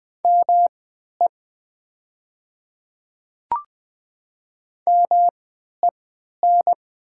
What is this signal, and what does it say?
Morse code, heard as a keyed tone, about 20 wpm, spaced as for 11 wpm, 700 Hz. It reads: ME MEN
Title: 3x Character Spacing - Top 200 Words - 20wpm